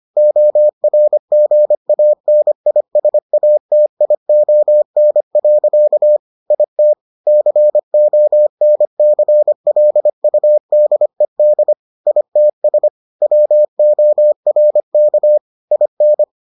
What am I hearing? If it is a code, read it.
Morse code, heard as a keyed tone, about 25 wpm, 600 Hz. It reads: ORGANISATION. IT CONCLUDED ITS WORK IN